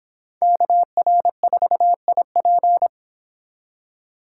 Morse code, heard as a keyed tone, about 26 wpm, 700 Hz. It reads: KR4IP